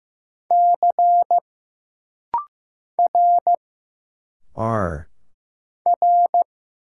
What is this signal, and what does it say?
Morse code, heard as a keyed tone, about 15 wpm, 700 Hz. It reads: C R R
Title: Single Letter-Number Alt - Rapid-Fire 15wpm